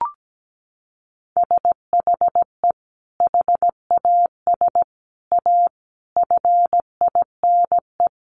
Morse code, heard as a keyed tone, about 17 wpm, 700 Hz. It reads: SHE HAS A FINE
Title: Sentences from Top 1200 Words 17wpm